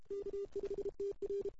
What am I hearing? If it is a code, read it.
Morse code, heard as a keyed tone, about 32 wpm, 400 Hz. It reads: K5TR
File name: CW / 2008-ss-cw-k5tr-op-n5rz -00:00